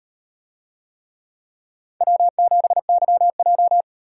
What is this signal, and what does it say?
Morse code, heard as a keyed tone, about 38 wpm, 700 Hz. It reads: W7YJ